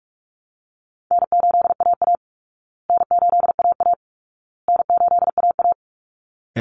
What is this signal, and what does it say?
Morse code, heard as a keyed tone, about 44 wpm, 700 Hz. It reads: N8AA N8AA N8AA
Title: Easy Call Signs Repeated 3x - 45wpm